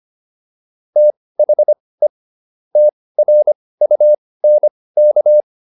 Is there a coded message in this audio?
Yes: Morse code from 1.0 to 5.4 s, heard as a keyed tone, about 25 wpm, spaced as for 18 wpm, 600 Hz. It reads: THE TRUNK